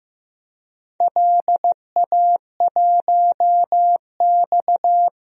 Morse code, heard as a keyed tone, about 15 wpm, 700 Hz. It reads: LA1X
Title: DXCC-100 Call Signs - 4 Characters - 15wpm